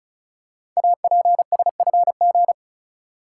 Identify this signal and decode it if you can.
Morse code, heard as a keyed tone, about 36 wpm, 690 Hz. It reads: APSFG
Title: Code Groups - 5 Letters - 35wpm